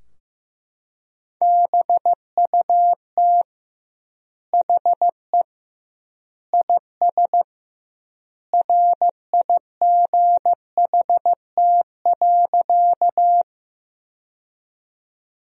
Morse code, heard as a keyed tone, about 15 wpm, 700 Hz. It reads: BUT HE IS RIGHT.
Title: 2x Word-Spacing - Sentences from Top 200 Words - 15wpm